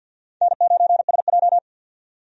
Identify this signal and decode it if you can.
Morse code, heard as a keyed tone, about 50 wpm, 700 Hz. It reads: N9SJ
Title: USA Call Signs - 4 Characters - 50wpm